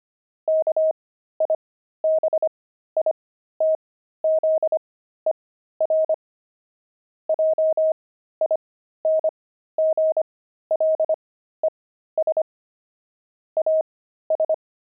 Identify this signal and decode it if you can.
Morse code, heard as a keyed tone, about 25 wpm, spaced as for 13 wpm, 630 Hz. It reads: KIBITZER JINGLES AS
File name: pangram-fs-13-cs-25-kibitzer